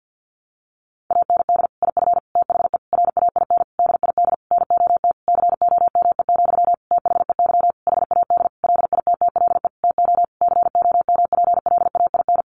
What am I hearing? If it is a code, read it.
Morse code, heard as a keyed tone, about 50 wpm, 700 Hz. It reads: AND IF THE RAIN DID NOT COME, THEY HAD LITTLE TO COMPLAIN
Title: Sentences from Top 1300 Words 50wpm